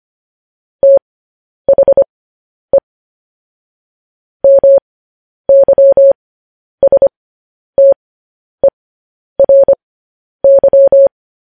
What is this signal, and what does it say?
Morse code, heard as a keyed tone, about 25 wpm, spaced as for 10 wpm, 550 Hz. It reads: THE MYSTERY
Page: news-headlines-in-morse-code-at-10-wpm